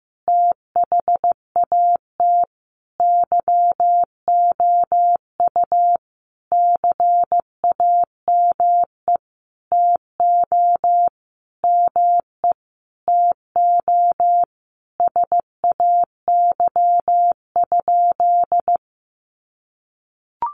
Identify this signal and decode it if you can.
Morse code, heard as a keyed tone, about 15 wpm, 700 Hz. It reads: THAT YOU CAME TO ME TO SAY?
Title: Sentences from Top 200 Words 15wpm